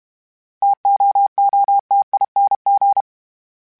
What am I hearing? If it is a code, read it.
Morse code, heard as a keyed tone, about 32 wpm, 800 Hz. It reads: TOOTING
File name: Difficult Combos / Difficult Combos-mp3s / T AOEI @30